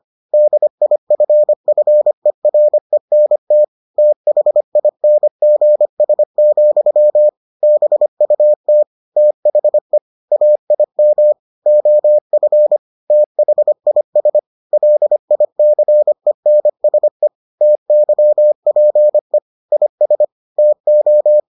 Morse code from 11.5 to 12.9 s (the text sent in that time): OF